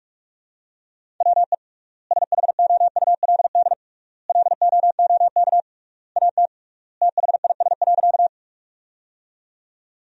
Morse code, heard as a keyed tone, about 46 wpm, 700 Hz. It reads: WE SHOULD LOOK AT THIS.
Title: 3x Word-Spacing - Sentences from Top 300 Words - 45wpm